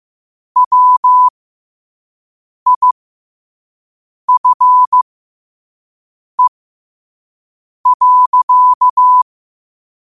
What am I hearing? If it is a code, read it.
Morse code, heard as a keyed tone, about 15 wpm, spaced as for 5 wpm, 1000 Hz. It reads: WIFE.
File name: qotdfiles / 5wpm